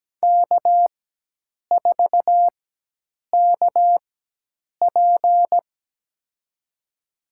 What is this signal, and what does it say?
Morse code, heard as a keyed tone, about 17 wpm, spaced as for 8 wpm, 700 Hz. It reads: K4KP
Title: USA Call Signs - 4 Characters - 4x Character Spacing - 17wpm